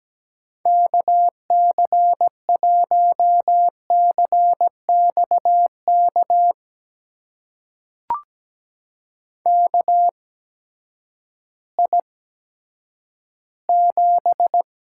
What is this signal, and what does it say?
Morse code, heard as a keyed tone, about 17 wpm, spaced as for 10 wpm, 700 Hz. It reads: KC1CXK KI7
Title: USA Call Signs - 6 Characters - 8x Character Spacing - 17wpm